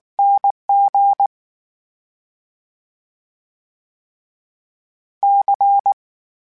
Morse code, heard as a keyed tone, about 19 wpm, 800 Hz. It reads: NG C
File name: Special INGs / Special INGs-mp3s / VerySlowToFaster Ing @18